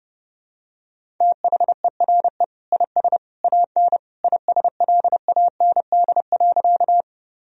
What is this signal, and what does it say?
Morse code, heard as a keyed tone, about 32 wpm, 700 Hz. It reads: THERE IS AN ISLAND.
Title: Sentences from Top 1000 Words - Review Part 3 - 30wpm